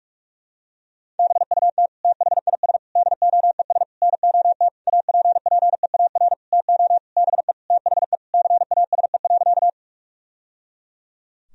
Morse code, heard as a keyed tone, about 46 wpm, 700 Hz. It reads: BUT THIS DOES NOT APPEAR TO BE THE CASE.
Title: Sentences from Top 1000 Words - Review Part 2 - 45wpm